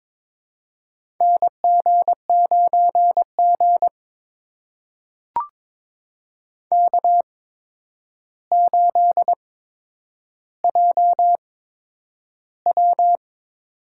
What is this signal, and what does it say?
Morse code, heard as a keyed tone, about 22 wpm, spaced as for 9 wpm, 700 Hz. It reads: NG9G K8JW
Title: USA Call Signs - 4 Characters - 8x Character Spacing - 22wpm